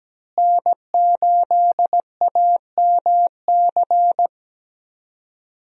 Morse code, heard as a keyed tone, about 17 wpm, 700 Hz. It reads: N8AMC
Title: USA Call Signs - 5 Characters - Repeated 3x - 17wpm